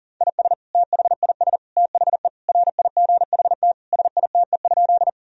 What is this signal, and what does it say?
Morse code, heard as a keyed tone, about 40 wpm, 700 Hz. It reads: IS THIS THE RIGHT SITE?
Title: Sentences from Top 1000 Words - Review Part 4 - 40wpm